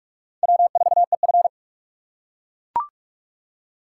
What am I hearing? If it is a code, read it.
Morse code, heard as a keyed tone, about 39 wpm, 700 Hz. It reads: W4EF